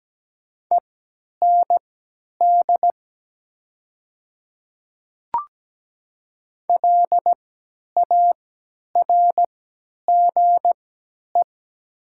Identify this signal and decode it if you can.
Morse code, heard as a keyed tone, about 17 wpm, spaced as for 10 wpm, 700 Hz. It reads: END LARGE
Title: 3x Character Spacing - Top 200 Words - 17wpm